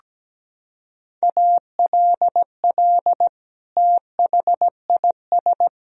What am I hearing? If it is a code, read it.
Morse code, heard as a keyed tone, about 17 wpm, 700 Hz. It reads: ALL THIS
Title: Sentences from Top 1000 Words - Review Part 4 - 17wpm